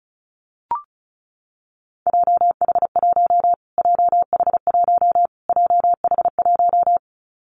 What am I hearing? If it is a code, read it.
Morse code, heard as a keyed tone, about 34 wpm, 700 Hz. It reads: JH1 JH1 JH1
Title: Callsign Prefixes for Top 25 DXCC - Repeated 3x - 35wpm